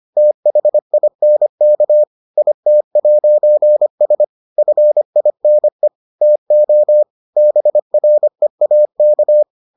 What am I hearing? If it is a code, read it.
Morse code, heard as a keyed tone, about 25 wpm, 600 Hz. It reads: THINK IT'S FINE TO BREAK